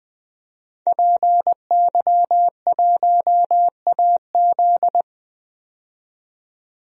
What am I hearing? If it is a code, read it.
Morse code, heard as a keyed tone, about 20 wpm, 700 Hz. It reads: PY1AZ
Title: DXCC-100 Call Signs - 5 Characters - 20wpm